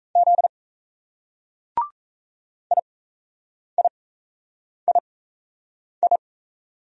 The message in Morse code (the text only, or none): Z IIII